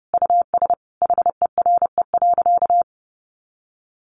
Morse code, heard as a keyed tone, about 30 wpm, 700 Hz. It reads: US HERE.
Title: A Sound Of Thunder 30wpm